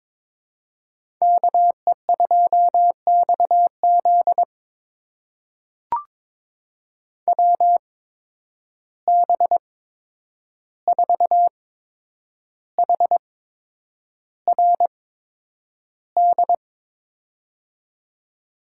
Morse code, heard as a keyed tone, about 22 wpm, spaced as for 9 wpm, 700 Hz. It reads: KE2XZ WB4HRD
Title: USA Call Signs - 8x Character Spacing - 22wpm